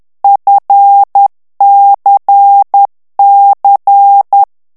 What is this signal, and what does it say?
Morse code, heard as a keyed tone, about 10 wpm, 800 Hz. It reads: FCC
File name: ANGLAIS texte clair 3 vitesse 10 mots minute